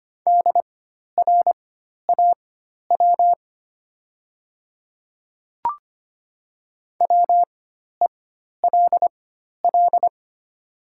Morse code, heard as a keyed tone, about 25 wpm, spaced as for 12 wpm, 700 Hz. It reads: DRAW WELL
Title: 4x Character Spacing - Top 500 Words - 25wpm